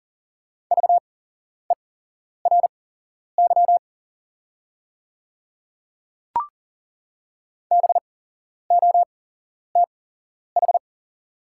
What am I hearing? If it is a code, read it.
Morse code, heard as a keyed tone, about 40 wpm, spaced as for 11 wpm, 690 Hz. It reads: VERY BOTH
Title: 8x Character Spacing - Top 300 Words - 40wpm